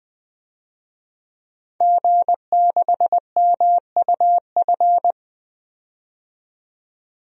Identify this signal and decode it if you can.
Morse code, heard as a keyed tone, about 20 wpm, 700 Hz. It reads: G6MUF
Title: DXCC-100 Call Signs - 5 Characters - 20wpm